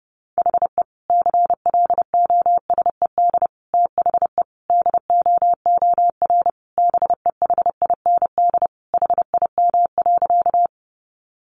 Morse code, heard as a keyed tone, about 30 wpm, 700 Hz. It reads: HE CLOSED THE DOOR BEHIND HIM.
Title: Sentences from Top 1400 Words 30wpm